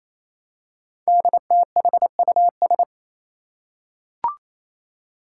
Morse code, heard as a keyed tone, about 29 wpm, 700 Hz. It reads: DTHUS
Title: Five Challenging Letter-Number Combinations 28wpm